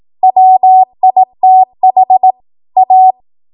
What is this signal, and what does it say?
Morse code, heard as a keyed tone, about 18 wpm, 750 Hz. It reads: WITH A
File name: Morse / 18WPM